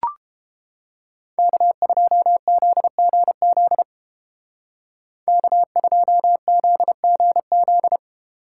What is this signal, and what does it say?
Morse code, heard as a keyed tone, about 30 wpm, 700 Hz. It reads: K2ZGZ K2ZGZ